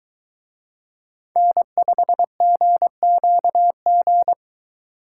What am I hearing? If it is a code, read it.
Morse code, heard as a keyed tone, about 22 wpm, 700 Hz. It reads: N5GQG